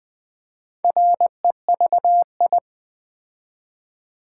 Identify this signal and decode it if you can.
Morse code, heard as a keyed tone, about 20 wpm, 700 Hz. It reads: REVI
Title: Build A Word - 2 Letters At A Time - 6-Letter Words - 20wpm